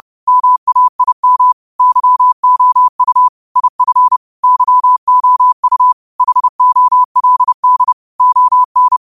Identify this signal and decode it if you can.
Morse code, heard as a keyed tone, about 30 wpm, 1000 Hz. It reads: MAIM YOU IF YOU HOLD ON
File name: qotdfiles / 30wpm